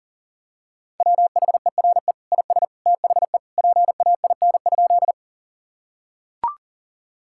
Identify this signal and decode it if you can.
Morse code, heard as a keyed tone, about 40 wpm, 700 Hz. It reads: WHERE IS THE PAIN?
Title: Sentences from Top 1600 Words 40wpm